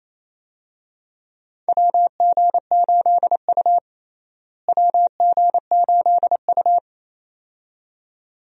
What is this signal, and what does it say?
Morse code, heard as a keyed tone, about 28 wpm, 700 Hz. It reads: WG8U WG8U
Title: USA Call Signs - 4 Characters - Repeated 2x - 28wpm